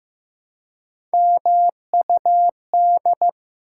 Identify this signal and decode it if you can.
Morse code, heard as a keyed tone, about 15 wpm, 700 Hz. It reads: MUD
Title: Three Letter Words - 15wpm